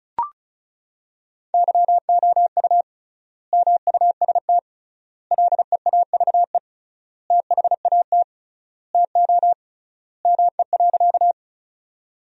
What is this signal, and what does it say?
Morse code, heard as a keyed tone, about 34 wpm, 700 Hz. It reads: YOU MUST LEAVE THAT TO ME.
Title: 3x Word-Spacing - Sentences from Top 400 Words - 35wpm